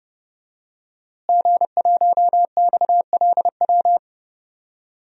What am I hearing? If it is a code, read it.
Morse code, heard as a keyed tone, about 32 wpm, 700 Hz. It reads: G1XLW
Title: DXCC-100 Call Signs - 5 Characters - Repeated 2x - 30wpm